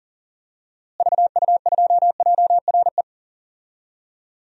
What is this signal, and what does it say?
Morse code, heard as a keyed tone, about 40 wpm, 700 Hz. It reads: VU2JRE
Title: DXCC-100 Call Signs - 6 Characters - Repeated 3x - 40wpm